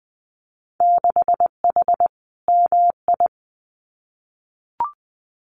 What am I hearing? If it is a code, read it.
Morse code, heard as a keyed tone, about 20 wpm, 700 Hz. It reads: 6H MI
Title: Winter Field Day Exchanges Encore 20wpm